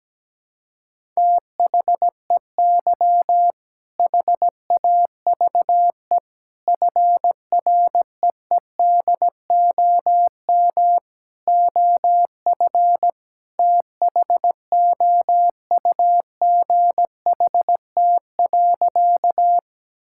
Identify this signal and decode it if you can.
Morse code, heard as a keyed tone, about 17 wpm, 700 Hz. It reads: THEY HAVE FREEDOM OF THOUGHT.
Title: Sentences from Top 1000 Words - Review Part 5 - 17wpm